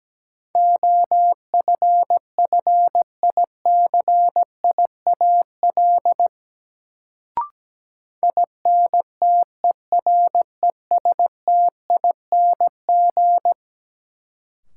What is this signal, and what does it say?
Morse code, heard as a keyed tone, about 17 wpm, 700 Hz. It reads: OFFICIAL INTERESTING
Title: Sentences from Top 1000 Words - New Word Drill - 17wpm